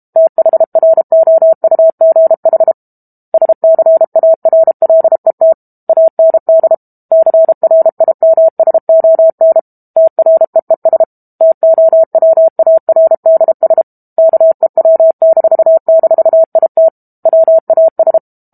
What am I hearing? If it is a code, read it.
Morse code, heard as a keyed tone, about 32 wpm, 650 Hz. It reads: THROUGH SCARLET AND CRIMSON TREES TOWARDS KEW--IT WAS
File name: Chapter 6 - 32 WPM